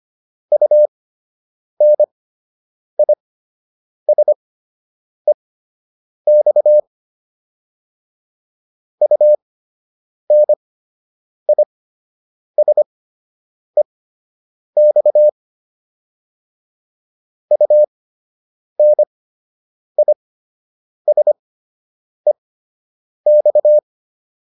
Morse code, heard as a keyed tone, about 25 wpm, spaced as for 8 wpm, 600 Hz. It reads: UNISEX UNISEX UNISEX